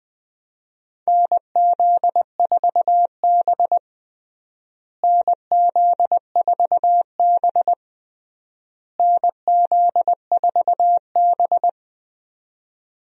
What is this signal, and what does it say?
Morse code, heard as a keyed tone, about 20 wpm, 700 Hz. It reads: NZ4B NZ4B NZ4B
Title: USA Call Signs - 4 Characters - Repeated 3x - 20wpm